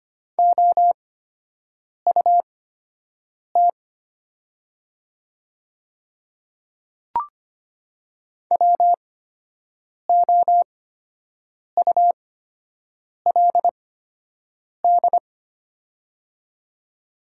Morse code, heard as a keyed tone, about 25 wpm, spaced as for 7 wpm, 700 Hz. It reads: OUT WOULD
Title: 8x Character Spacing - Top 100 Words - 25wpm